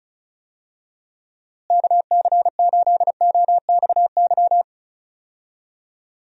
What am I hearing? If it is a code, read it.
Morse code, heard as a keyed tone, about 36 wpm, 700 Hz. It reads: KC8OXY
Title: USA Call Signs - 6 Characters - Repeated 3x - 35wpm